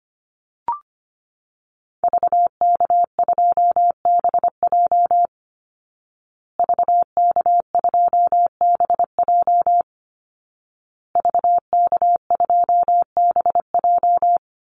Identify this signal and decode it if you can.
Morse code, heard as a keyed tone, about 25 wpm, 700 Hz. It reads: VK2BJ VK2BJ VK2BJ
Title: Call Signs Repeated 2x - 25wpm